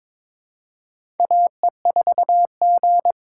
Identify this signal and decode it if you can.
Morse code, heard as a keyed tone, about 22 wpm, 700 Hz. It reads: AE4G
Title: USA Call Signs - 4 Characters - 2x Character Spacing - 22wpm